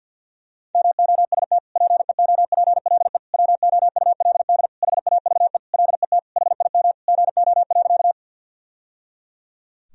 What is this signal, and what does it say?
Morse code, heard as a keyed tone, about 52 wpm, 700 Hz. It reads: MOST PEOPLE WOULD HAVE LET HIM GO.
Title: Sentences from Top 1000 Words - Review Part 1 - 50wpm